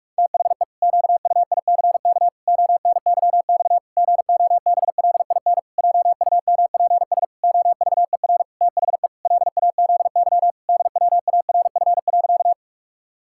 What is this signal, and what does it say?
Morse code, heard as a keyed tone, about 46 wpm, 700 Hz. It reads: THE QUICK ONYX GOBLIN JUMPS OVER THE LAZY DWARF.
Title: Pangram Challenge v2 - 45wpm